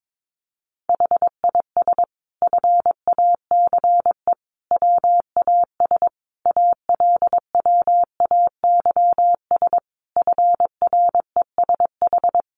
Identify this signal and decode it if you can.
Morse code, heard as a keyed tone, about 22 wpm, 700 Hz. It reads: HIS FACE WAS ALWAYS FRESH
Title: Sentences from Top 1200 Words 22wpm